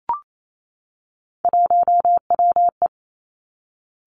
Morse code, heard as a keyed tone, about 28 wpm, 690 Hz. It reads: 1WE
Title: Three Challenging Letter-Number Combinations 28wpm